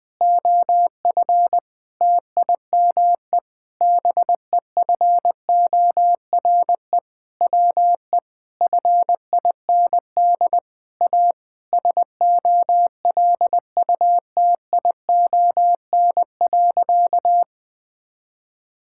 Morse code, heard as a keyed tone, about 20 wpm, 700 Hz. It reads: OF TIME BEFORE WE FIND A SOLUTION.
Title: Sentences with 8-Word Phrases - 20wpm